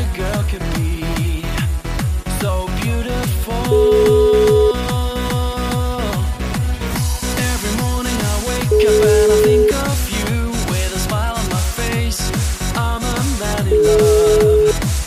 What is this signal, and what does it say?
Ringback tone (Europe, 425 Hz), with noise.